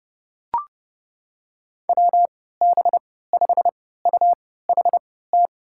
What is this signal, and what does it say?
Morse code, heard as a keyed tone, about 30 wpm, spaced as for 17 wpm, 700 Hz. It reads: WB5UHT
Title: USA Call Signs - 3x Character Spacing - 30wpm